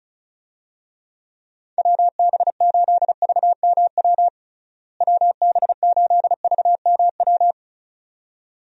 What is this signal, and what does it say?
Morse code, heard as a keyed tone, about 36 wpm, 700 Hz. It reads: WB8VMW WB8VMW
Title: USA Call Signs - 6 Characters - Repeated 2x - 35wpm